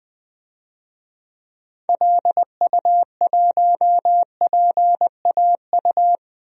Morse code, heard as a keyed tone, about 20 wpm, 700 Hz. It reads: LU1PAU
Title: DXCC-100 Call Signs - 6 Characters - 20wpm